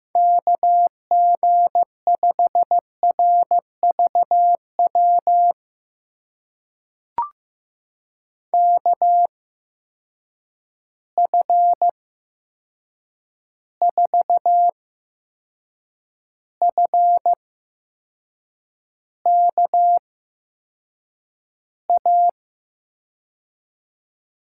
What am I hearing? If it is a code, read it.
Morse code, heard as a keyed tone, about 15 wpm, spaced as for 6 wpm, 700 Hz. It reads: KG5RVW KF4FKA
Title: USA Call Signs - 6 Characters - 8x Character Spacing - 15wpm